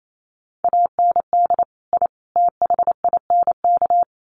Morse code, heard as a keyed tone, about 28 wpm, 700 Hz. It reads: AND I THINK
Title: Sets of 3 Words 28wpm